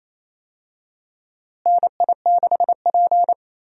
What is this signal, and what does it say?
Morse code, heard as a keyed tone, about 28 wpm, 700 Hz. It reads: NI6P
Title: USA Call Signs - 4 Characters - Repeated 2x - 28wpm